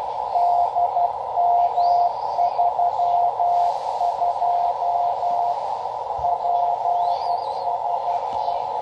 Morse code, heard as a keyed tone, about 12 wpm, 700 Hz, with radio noise. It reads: DM32 PSE
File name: Half Watt 10 Meter Beacon